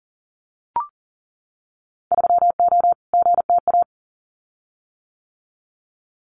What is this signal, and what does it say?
Morse code, heard as a keyed tone, about 40 wpm, 700 Hz. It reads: 3O GTA
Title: Winter Field Day Exchanges Encore 40wpm